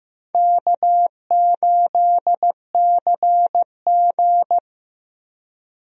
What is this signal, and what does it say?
Morse code, heard as a keyed tone, about 15 wpm, 700 Hz. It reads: K8CG
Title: USA Call Signs - 4 Characters - Repeated 3x - 15wpm